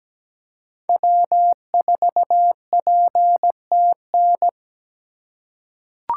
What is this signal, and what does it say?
Morse code, heard as a keyed tone, about 17 wpm, 700 Hz. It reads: W4PTN
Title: USA Call Signs - 6x Character Spacing - 17wpm